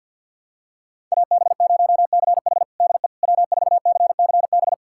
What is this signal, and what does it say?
Morse code, heard as a keyed tone, about 51 wpm, 700 Hz. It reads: AB0CH DE W4CCB
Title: USA Callsign DE Callsign - 5 Characters - 50wpm